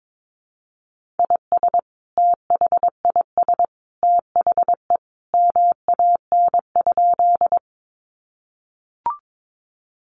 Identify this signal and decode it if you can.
Morse code, heard as a keyed tone, about 22 wpm, 700 Hz. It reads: IS THIS THE MAN?
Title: Sentences from Top 200 Words 22wpm